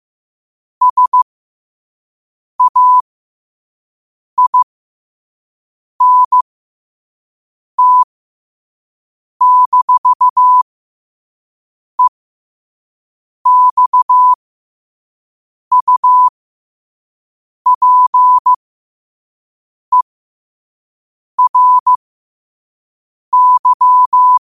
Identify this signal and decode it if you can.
Morse code, heard as a keyed tone, about 15 wpm, spaced as for 5 wpm, 1000 Hz. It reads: SAINT-EXUPERY
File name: qotdfiles / 5wpm